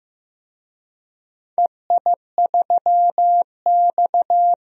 Morse code, heard as a keyed tone, about 15 wpm, 700 Hz. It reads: EI3X
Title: DXCC-100 Call Signs - 4 Characters - 15wpm